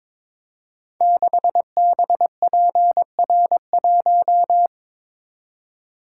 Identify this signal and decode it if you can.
Morse code, heard as a keyed tone, about 22 wpm, 700 Hz. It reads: 6BPR1
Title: Code Groups - 5 Letter:Numbers - 22wpm